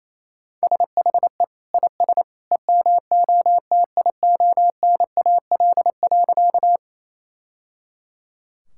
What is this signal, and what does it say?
Morse code, heard as a keyed tone, about 28 wpm, 700 Hz. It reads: SHE IS EMOTIONAL.
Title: Sentences from Top 1000 Words - Review Part 5 - 28wpm